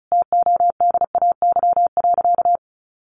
Morse code, heard as a keyed tone, about 34 wpm, 700 Hz. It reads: TODAY.
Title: Sentences from Top 600 Words 35wpm